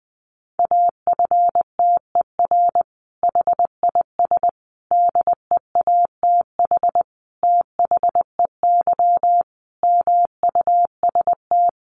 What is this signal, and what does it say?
Morse code, heard as a keyed tone, about 19 wpm, 700 Hz. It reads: AFTER HIS DEATH THEY MUST
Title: Sentences from Top 1100 Words 20wpm